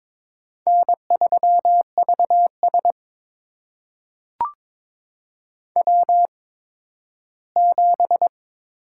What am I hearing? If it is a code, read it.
Morse code, heard as a keyed tone, about 22 wpm, spaced as for 13 wpm, 700 Hz. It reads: N3VS W7
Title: USA Call Signs - 8x Character Spacing - 22wpm